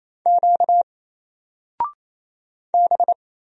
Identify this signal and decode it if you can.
Morse code, heard as a keyed tone, about 27 wpm, 700 Hz. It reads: Q B
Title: Single Character (Letter:Number:Punctuation:Prosign) Alt 28wpm